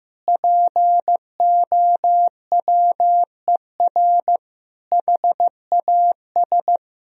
Morse code, heard as a keyed tone, about 15 wpm, 700 Hz. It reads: POWER HAS
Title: Sentences from Top 1000 Words - Review Part 2 - 15wpm